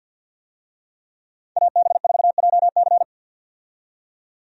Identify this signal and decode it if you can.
Morse code, heard as a keyed tone, about 46 wpm, 700 Hz. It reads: AB4JC
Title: USA Call Signs - 5 Characters - 50wpm